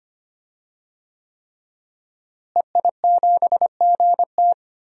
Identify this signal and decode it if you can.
Morse code, heard as a keyed tone, about 25 wpm, 700 Hz. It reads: EI7GT
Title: DXCC-100 Call Signs - 5 Characters - Repeated 3x - 25wpm